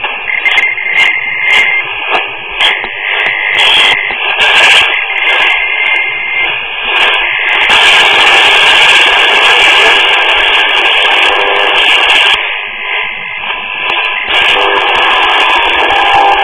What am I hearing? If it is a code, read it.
Dial-up modem handshake.